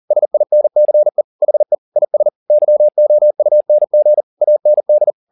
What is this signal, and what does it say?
Morse code, heard as a keyed tone, about 40 wpm, 600 Hz. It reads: SINCE HE IS YOUNG AND